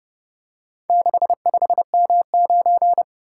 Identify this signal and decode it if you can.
Morse code, heard as a keyed tone, about 29 wpm, 700 Hz. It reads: 65M9